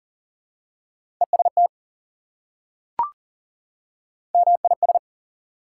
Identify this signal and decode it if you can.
Morse code, heard as a keyed tone, about 40 wpm, 700 Hz. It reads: EST MIS
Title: Prefixes and Suffixes 40wpm